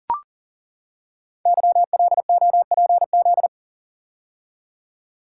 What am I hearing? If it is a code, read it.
Morse code, heard as a keyed tone, about 38 wpm, 690 Hz. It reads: YLOPZ
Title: Code Groups - 5 Letters - 40wpm